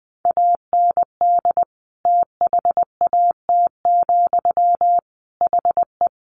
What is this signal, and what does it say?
Morse code, heard as a keyed tone, about 20 wpm, 700 Hz. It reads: AND THAT, HE
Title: A Sound Of Thunder 20wpm